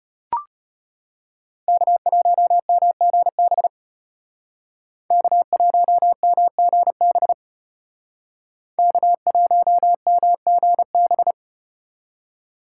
Morse code, heard as a keyed tone, about 35 wpm, 700 Hz. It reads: K1MGB K1MGB K1MGB